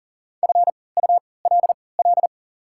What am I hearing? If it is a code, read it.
Morse code, heard as a keyed tone, about 37 wpm, spaced as for 22 wpm, 700 Hz. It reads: FULL
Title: 3x Character Spacing - Top 500 Words - 40wpm